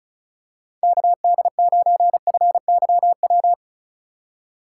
Morse code, heard as a keyed tone, about 36 wpm, 700 Hz. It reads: KD9FYW
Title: USA Call Signs - 6 Characters - 3x Character Spacing - 35wpm